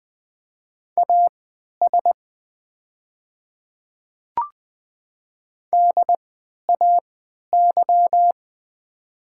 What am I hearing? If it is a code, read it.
Morse code, heard as a keyed tone, about 20 wpm, spaced as for 11 wpm, 690 Hz. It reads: AS DAY
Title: 3x Character Spacing - Top 200 Words - 20wpm